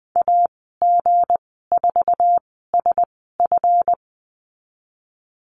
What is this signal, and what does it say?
Morse code, heard as a keyed tone, about 20 wpm, spaced as for 15 wpm, 700 Hz. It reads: AG4SF
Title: 186 Lesson 30g - Letter b - Call Signs 20wpm